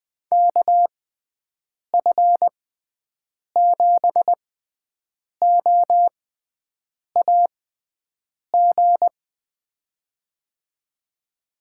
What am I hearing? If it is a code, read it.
Morse code, heard as a keyed tone, about 21 wpm, spaced as for 7 wpm, 700 Hz. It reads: KF7OAG
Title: USA Call Signs - 6 Characters - 6x Character Spacing - 20wpm